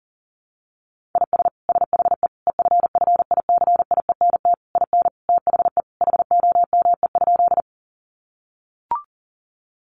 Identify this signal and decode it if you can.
Morse code, heard as a keyed tone, about 40 wpm, 700 Hz. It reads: IS SHE EFFICIENT IN THE HOME?
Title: Sentences from Top 1300 Words 40wpm